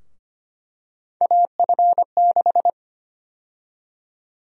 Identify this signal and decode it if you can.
Morse code, heard as a keyed tone, about 25 wpm, 700 Hz. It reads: AF6
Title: USA Call Sign Prefixes - 25wpm